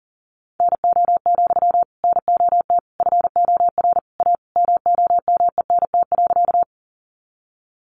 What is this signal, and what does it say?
Morse code, heard as a keyed tone, about 40 wpm, 700 Hz. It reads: NO, NOT FOR A MOMENT.
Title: Sentences from Top 1000 Words 40wpm